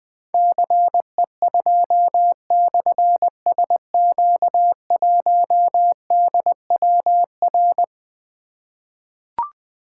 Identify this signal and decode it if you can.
Morse code, heard as a keyed tone, about 20 wpm, 700 Hz. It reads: CE2/SQ1DWR
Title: Difficult Call Signs 20wpm